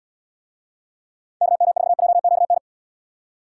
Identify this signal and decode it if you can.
Morse code, heard as a keyed tone, about 68 wpm, 700 Hz. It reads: KM4JQN